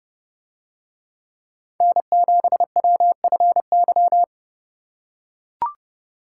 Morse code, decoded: N7WFY

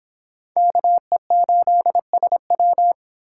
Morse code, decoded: KE8SW

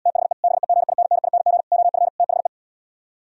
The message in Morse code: THE BEGINNING OF THE

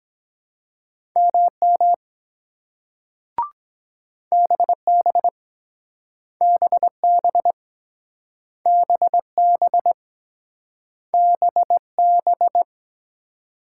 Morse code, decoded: MM BB BB BB BB

